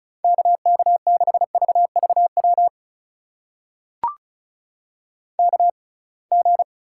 Morse code, heard as a keyed tone, about 34 wpm, spaced as for 26 wpm, 700 Hz. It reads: KK6VVW KG
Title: USA Call Signs - 6 Characters - 6x Character Spacing - 35wpm